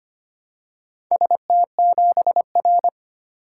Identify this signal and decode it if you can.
Morse code, heard as a keyed tone, about 25 wpm, 700 Hz. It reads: ST7R
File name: practice / 25wpm / Four Letter-Number Combinations - 25wpm